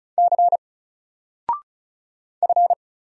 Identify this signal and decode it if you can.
Morse code, heard as a keyed tone, about 36 wpm, 700 Hz. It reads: C F
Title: Single Letter-Number Alt 35wpm